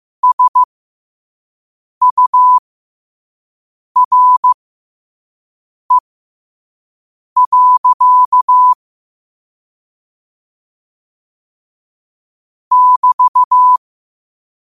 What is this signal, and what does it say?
Morse code, heard as a keyed tone, about 15 wpm, spaced as for 5 wpm, 1000 Hz. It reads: SURE. <BT>
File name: qotdfiles / 5wpm